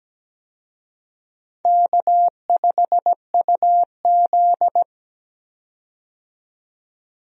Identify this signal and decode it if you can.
Morse code, heard as a keyed tone, about 17 wpm, 700 Hz. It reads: K5UZ
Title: USA Call Signs - 4 Characters - 17wpm